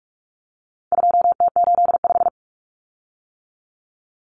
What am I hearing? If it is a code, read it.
Morse code, heard as a keyed tone, about 44 wpm, 700 Hz. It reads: 2T85
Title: Common Cut Numbers - 4-Digit 45wpm